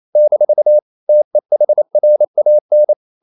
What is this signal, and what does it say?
Morse code, heard as a keyed tone, about 28 wpm, 600 Hz. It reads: - TEHRAN
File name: CW / Practice / NationalCapitals28